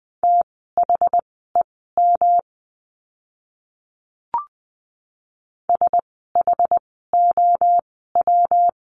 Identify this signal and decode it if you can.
Morse code, heard as a keyed tone, about 20 wpm, spaced as for 14 wpm, 700 Hz. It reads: THEM SHOW
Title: 230 Lesson 37f - Letter x - Words 20wpm